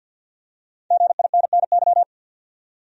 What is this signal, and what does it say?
Morse code, heard as a keyed tone, about 51 wpm, 700 Hz. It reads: GINNY
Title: USA Common Names - 50wpm